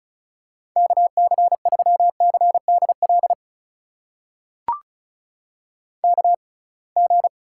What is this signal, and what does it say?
Morse code, heard as a keyed tone, about 36 wpm, spaced as for 27 wpm, 700 Hz. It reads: KC3CDL KG
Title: USA Call Signs - 6 Characters - 6x Character Spacing - 35wpm